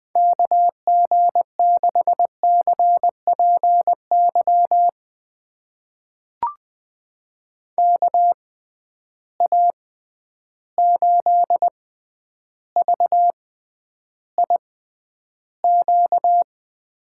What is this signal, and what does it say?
Morse code, heard as a keyed tone, about 20 wpm, spaced as for 10 wpm, 700 Hz. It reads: KG6CPY KA8VIQ
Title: USA Call Signs - 6 Characters - 6x Character Spacing - 20wpm